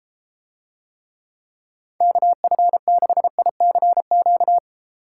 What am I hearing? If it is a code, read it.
Morse code, heard as a keyed tone, about 33 wpm, 700 Hz. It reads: KF6ICQ